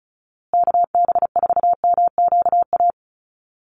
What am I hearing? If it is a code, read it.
Morse code, heard as a keyed tone, about 34 wpm, 700 Hz. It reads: KB4MQA
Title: USA Call Signs Repeated 3x 35wpm